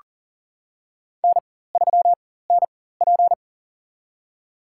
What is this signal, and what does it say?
Morse code, heard as a keyed tone, about 40 wpm, spaced as for 19 wpm, 700 Hz. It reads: N3NP
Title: USA Call Signs - 4 Characters - 4x Character Spacing - 40wpm